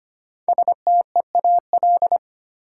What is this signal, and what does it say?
Morse code, heard as a keyed tone, about 25 wpm, 700 Hz. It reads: STEAL
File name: 5 Letter Words - First 3 Letters Given - 25wpm